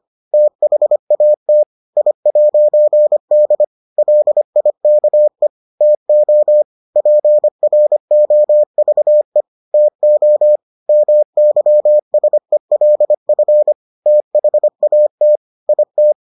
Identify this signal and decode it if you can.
Morse code, heard as a keyed tone, about 25 wpm, 600 Hz. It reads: THAT I'D LIKE TO PROVE TO MYSELF THAT IT